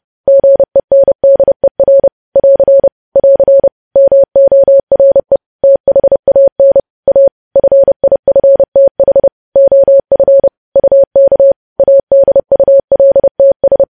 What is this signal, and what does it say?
Morse code, heard as a keyed tone, about 30 wpm, 550 Hz. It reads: GENDER <AR> <AR> MORE THAN A FIFTH OF UK ADULTS
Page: news headlines in morse code at 30 wpm